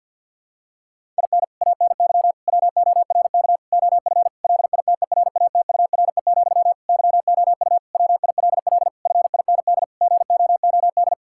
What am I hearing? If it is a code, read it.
Morse code, heard as a keyed tone, about 49 wpm, 700 Hz. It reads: IN ANY WORK OF LITERATURE, YOU WILL FIND GOOD
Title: Sentences from Top 1000 Words - Review Part 6 - 50wpm